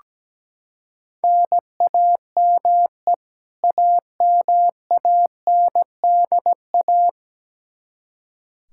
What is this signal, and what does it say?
Morse code, heard as a keyed tone, about 17 wpm, 700 Hz. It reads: NAME AMANDA
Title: Practice QSO - Exchange 1 - Beginning - 17wpm